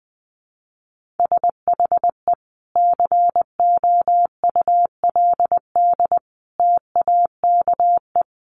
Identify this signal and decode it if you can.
Morse code, heard as a keyed tone, about 20 wpm, 700 Hz. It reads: SHE COULD TAKE
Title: Sentences from Top 900 Words 20wpm